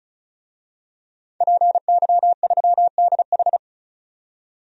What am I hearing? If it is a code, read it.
Morse code, heard as a keyed tone, about 36 wpm, 700 Hz. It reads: PY3DH
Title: DXCC-100 Call Signs - 5 Characters - 35wpm